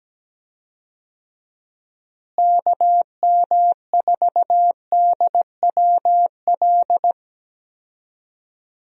Morse code, heard as a keyed tone, about 17 wpm, 700 Hz. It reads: KM4DWL
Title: USA Call Signs - 6 Characters - Repeated 2x - 17wpm